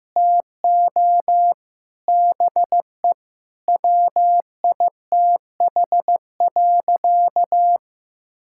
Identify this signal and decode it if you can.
Morse code, heard as a keyed tone, about 16 wpm, 700 Hz. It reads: TO BE WITH.
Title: Sentences from Top 1000 Words - Review Part 5 - 15wpm